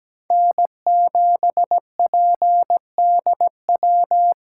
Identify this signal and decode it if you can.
Morse code, heard as a keyed tone, about 17 wpm, 700 Hz. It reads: N7PDW
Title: USA Call Signs - 5 Characters - 3x Character Spacing - 17wpm